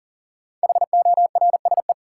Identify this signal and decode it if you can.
Morse code, heard as a keyed tone, about 40 wpm, 700 Hz. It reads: HORSE
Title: Sentences from Top 300 Words - New Word Drill - 40wpm